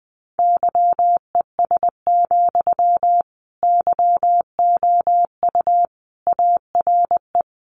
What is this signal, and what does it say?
Morse code, heard as a keyed tone, about 20 wpm, 700 Hz. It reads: YES, YOU ARE
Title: Sentences from Top 500 Words 20wpm